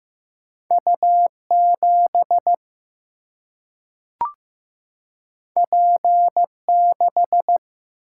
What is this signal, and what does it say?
Morse code, heard as a keyed tone, about 15 wpm, 700 Hz. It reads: U 7 P 6
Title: Single-Letter Single-Number 15wpm